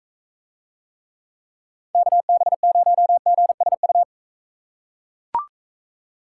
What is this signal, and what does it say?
Morse code, heard as a keyed tone, about 43 wpm, 700 Hz. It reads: KB0GSU